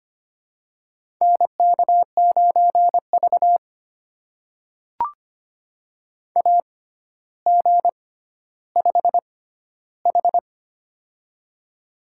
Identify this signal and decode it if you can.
Morse code, heard as a keyed tone, about 25 wpm, spaced as for 13 wpm, 700 Hz. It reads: NK9V AG5H
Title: USA Call Signs - 4 Characters - 6x Character Spacing - 25wpm